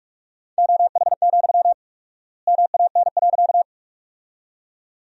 Morse code, heard as a keyed tone, about 46 wpm, 700 Hz. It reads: OH, MAN.
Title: 4x Word-Spacing - Sentences from Top 500 Words - 45wpm